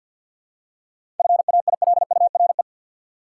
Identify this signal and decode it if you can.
Morse code, heard as a keyed tone, about 48 wpm, 700 Hz. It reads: FAILURE
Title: Seven Letter Words - 50wpm